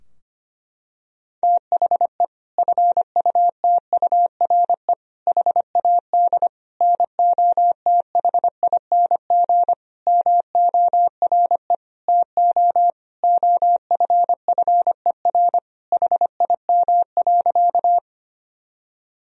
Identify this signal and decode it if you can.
Morse code, heard as a keyed tone, about 25 wpm, 700 Hz. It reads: THE FUTURE HAD NOTHING MORE TO OFFER HIM.
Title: Sentences from Top 1000 Words - Review Part 4 - 25wpm